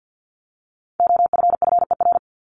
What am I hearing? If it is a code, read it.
Morse code, heard as a keyed tone, about 50 wpm, 700 Hz. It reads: OFFER
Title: Top 501-750 Words 50wpm